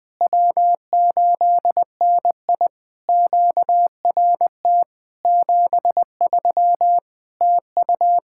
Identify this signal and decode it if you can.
Morse code, heard as a keyed tone, about 20 wpm, 700 Hz. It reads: W8NI QRT 73 TU
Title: POTA QSO Practice - 20wpm